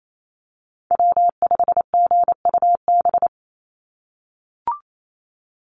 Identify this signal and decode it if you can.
Morse code, heard as a keyed tone, about 28 wpm, 700 Hz. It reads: W5GUB
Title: USA Call Signs 28wpm